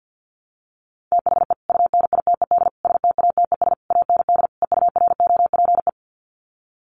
Morse code, heard as a keyed tone, about 48 wpm, 700 Hz. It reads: THE UNITED STATES AND EUROPE
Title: Sets of 5 Words Encore 50wpm